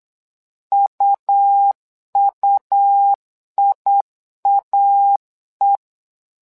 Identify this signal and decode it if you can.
Morse code, heard as a keyed tone, about 8 wpm, 800 Hz. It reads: UUIAE
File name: LEZIONI / 2 UHDN / 2 UH 8wpm 2129